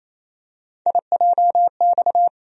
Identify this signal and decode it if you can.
Morse code, heard as a keyed tone, about 28 wpm, 700 Hz. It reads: IJX
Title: USA Call Sign Postfixes - 3 Letters 28wpm